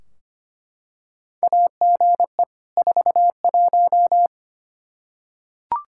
Morse code, heard as a keyed tone, about 25 wpm, 700 Hz. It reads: AGE 41
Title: Practice QSO - Exchange 3 - More Info - 25wpm